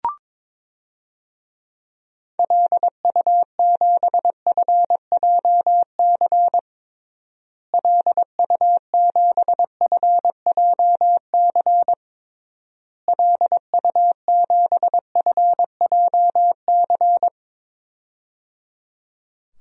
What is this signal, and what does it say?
Morse code, heard as a keyed tone, about 22 wpm, 700 Hz. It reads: LU7FJC LU7FJC LU7FJC
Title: DXCC-100 Call Signs - 6 Characters - Repeated 3x - 22wpm